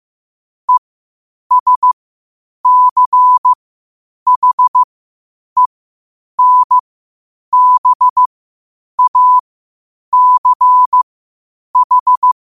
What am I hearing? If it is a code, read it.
Morse code, heard as a keyed tone, about 15 wpm, spaced as for 8 wpm, 1000 Hz. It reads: ESCHENBACH
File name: qotdfiles / 8wpm